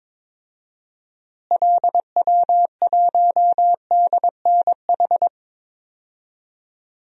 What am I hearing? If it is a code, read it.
Morse code, heard as a keyed tone, about 22 wpm, 700 Hz. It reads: LW1DNH
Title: DXCC-100 Call Signs - 6 Characters - Repeated 2x - 22wpm